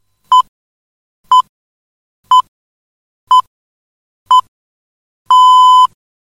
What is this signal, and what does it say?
Radio time signal (time pips). It is heard from 0.3 to 5.9 s.
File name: rg / cuñas / Radio Guadalix - señal h